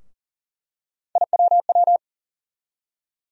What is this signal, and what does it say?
Morse code, heard as a keyed tone, about 40 wpm, 700 Hz. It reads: IWW
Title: USA Call Sign Postfixes - 3 Letters 40wpm